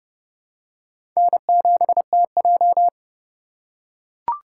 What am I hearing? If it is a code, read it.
Morse code, heard as a keyed tone, about 30 wpm, 700 Hz. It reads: N7TJ
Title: USA Call Signs - 3x Character Spacing - 30wpm